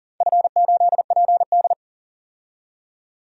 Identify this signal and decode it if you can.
Morse code, heard as a keyed tone, about 40 wpm, 700 Hz. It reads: F8PD
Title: DXCC-100 Call Signs - 4 Characters - 40wpm